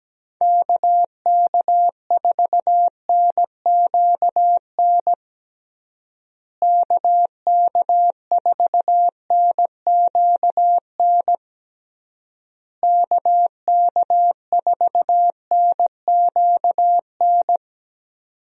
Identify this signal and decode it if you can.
Morse code, heard as a keyed tone, about 17 wpm, 700 Hz. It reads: KK4NQN KK4NQN KK4NQN
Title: USA Call Signs - 6 Characters - Repeated 3x - 17wpm